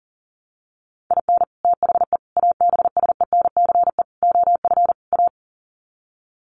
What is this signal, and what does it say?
Morse code, heard as a keyed tone, about 40 wpm, 700 Hz. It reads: IN THE ABSENCE OF A
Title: Sets of 5 Words Encore 40wpm